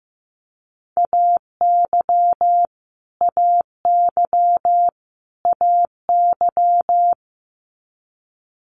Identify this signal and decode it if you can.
Morse code, heard as a keyed tone, about 15 wpm, 700 Hz. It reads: AY AY AY
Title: Multi-letter Phonemes Repeated 3x 15wpm